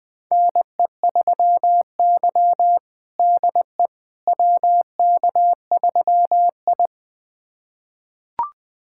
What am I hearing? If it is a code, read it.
Morse code, heard as a keyed tone, about 20 wpm, 700 Hz. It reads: NE3Y DE WK3I
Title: USA Callsign DE Callsign - 4 Characters - 20wpm